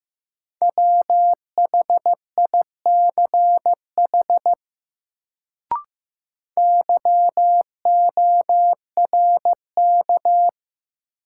Morse code, heard as a keyed tone, about 15 wpm, 700 Hz. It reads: WHICH YORK
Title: Sets of 2 Words - New Word Drill - 15wpm